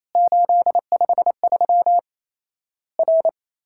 Morse code, heard as a keyed tone, about 28 wpm, 700 Hz. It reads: 853 R
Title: Worked All Europe DX Contest - QTC Exchange - 28wpm